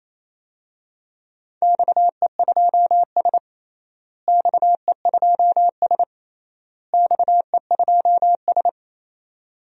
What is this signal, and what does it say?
Morse code, heard as a keyed tone, about 28 wpm, 700 Hz. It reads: XE2S XE2S XE2S
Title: DXCC-100 Call Signs - 4 Characters - Repeated 3x - 28wpm